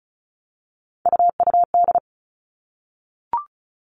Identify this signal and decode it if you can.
Morse code, heard as a keyed tone, about 34 wpm, 700 Hz. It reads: UUD
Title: All Cut Numbers - 3-Digit 35wpm